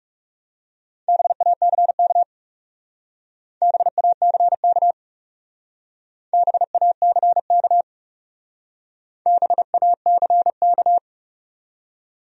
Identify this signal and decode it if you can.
Morse code, heard as a keyed tone, about 36 wpm, 700 Hz. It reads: BACK BACK BACK BACK
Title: Speed-Racing Top 200 Words 30wpm